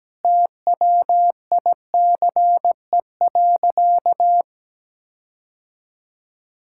Morse code, heard as a keyed tone, about 17 wpm, 700 Hz. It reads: TWICE.
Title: Sentences from Top 1600 Words 17wpm